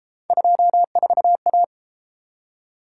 Morse code, heard as a keyed tone, about 29 wpm, 700 Hz. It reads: 24A